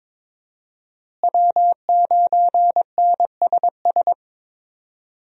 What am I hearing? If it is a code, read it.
Morse code, heard as a keyed tone, about 22 wpm, 700 Hz. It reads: W9NSS
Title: USA Call Signs - 5 Characters - Repeated 3x - 22wpm